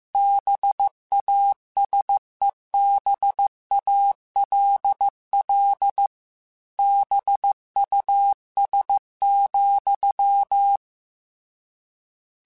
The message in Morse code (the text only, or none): BASEBALL BUS,